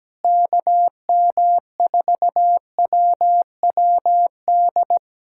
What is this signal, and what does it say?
Morse code, heard as a keyed tone, about 17 wpm, 700 Hz. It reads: KM4WWD
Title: USA Callsign DE Callsign - 6 Characters - 17wpm